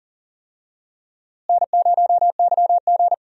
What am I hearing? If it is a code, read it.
Morse code, heard as a keyed tone, about 38 wpm, 700 Hz. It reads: N0YG
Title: USA Call Signs - 4 Characters - Repeated 2x - 40wpm